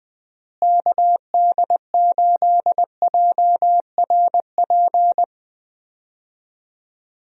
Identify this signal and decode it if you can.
Morse code, heard as a keyed tone, about 20 wpm, 700 Hz. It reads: KD8JRP
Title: USA Call Signs - 6 Characters - Repeated 3x - 20wpm